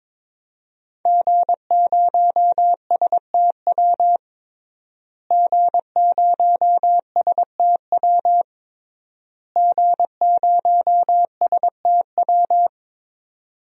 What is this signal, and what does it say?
Morse code, heard as a keyed tone, about 22 wpm, 700 Hz. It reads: G0STW G0STW G0STW
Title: DXCC-100 Call Signs - 5 Characters - Repeated 3x - 22wpm